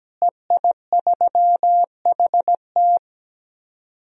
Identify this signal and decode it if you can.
Morse code, heard as a keyed tone, about 17 wpm, 700 Hz. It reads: EI3HT
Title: DXCC-100 Call Signs - 5 Characters - Repeated 3x - 17wpm